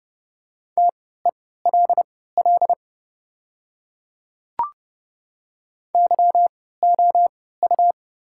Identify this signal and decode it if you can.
Morse code, heard as a keyed tone, about 30 wpm, spaced as for 17 wpm, 700 Hz. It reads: TELL YOU
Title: 3x Character Spacing - Top 500 Words - 30wpm